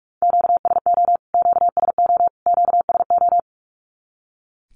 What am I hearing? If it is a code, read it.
Morse code, heard as a keyed tone, about 44 wpm, 700 Hz. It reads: QSO QSO QSO
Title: Q-Codes Repeated 3x 45wpm